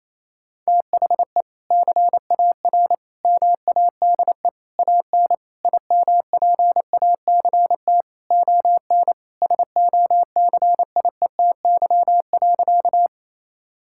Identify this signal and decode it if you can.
Morse code, heard as a keyed tone, about 28 wpm, 700 Hz. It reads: THE CAR MADE AN IMPACT ON SOCIETY.
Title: Sentences from Top 1000 Words - Review Part 6 - 28wpm